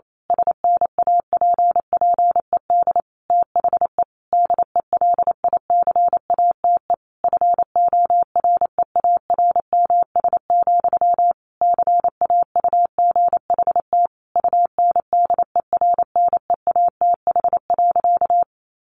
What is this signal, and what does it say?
Morse code, heard as a keyed tone, about 28 wpm, 700 Hz. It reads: SNAPPED THE DELICATE FOREARMS, CAUGHT UNDERNEATH.
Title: A Sound Of Thunder 28wpm